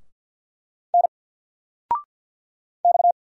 Morse code, heard as a keyed tone, about 50 wpm, 700 Hz. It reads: N X
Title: Single Letter-Number 50wpm